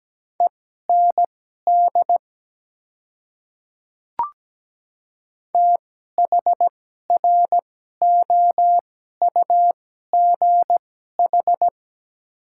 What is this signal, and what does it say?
Morse code, heard as a keyed tone, about 17 wpm, spaced as for 12 wpm, 700 Hz. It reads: END THROUGH
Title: 2x Character Spacing - Top 200 Words - 17wpm